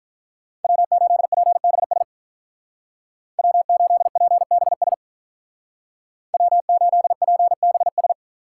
Morse code, heard as a keyed tone, about 47 wpm, 700 Hz. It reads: W8PBS W8PBS W8PBS